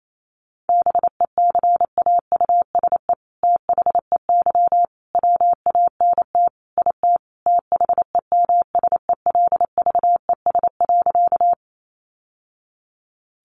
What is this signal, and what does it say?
Morse code, heard as a keyed tone, about 28 wpm, 700 Hz. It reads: BECAUSE THEY WANT IT THEMSELVES.
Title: Sentences from Top 800 Words 28wpm